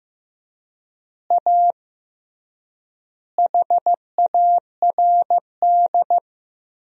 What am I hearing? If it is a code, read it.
Morse code, heard as a keyed tone, about 14 wpm, 700 Hz. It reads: A HARD
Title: 3x Word-Spacing - Sentences from Top 300 Words - 15wpm